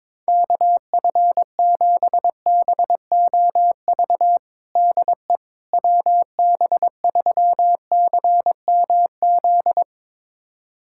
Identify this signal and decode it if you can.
Morse code, heard as a keyed tone, about 22 wpm, 700 Hz. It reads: KF7BOV DE WB3CMZ
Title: USA Callsign DE Callsign 22wpm